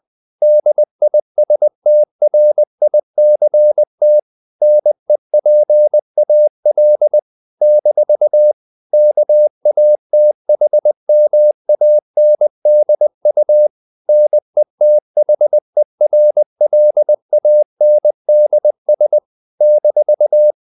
Morse code, heard as a keyed tone, about 20 wpm, 600 Hz. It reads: DISTRICT NEPAL - KATHMANDU NETHERLANDS -
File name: CW / Practice / NationalCapitals20